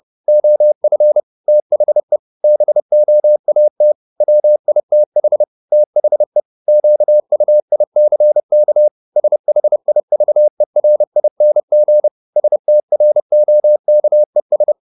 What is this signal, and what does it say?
Morse code, heard as a keyed tone, about 30 wpm, 600 Hz. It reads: OF THE BOAT WITH THE QUICK SHIVERING STROKES
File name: the old man and the sea / the old man and the sea 1 30wpm